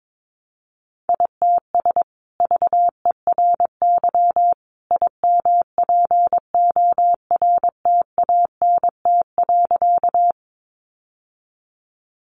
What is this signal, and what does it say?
Morse code, heard as a keyed tone, about 22 wpm, 700 Hz. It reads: ITS VERY IMPORTANT.
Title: Sentences from Top 600 Words 22wpm